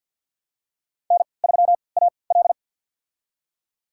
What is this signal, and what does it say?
Morse code, heard as a keyed tone, about 51 wpm, spaced as for 28 wpm, 700 Hz. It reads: N3AL
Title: USA Call Signs - 3x Character Spacing - 50wpm